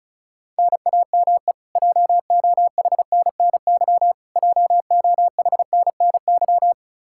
Morse code, heard as a keyed tone, about 36 wpm, 700 Hz. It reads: NAME JOHNNY JOHNNY
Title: Practice QSO - Exchange 1 - Beginning - 35wpm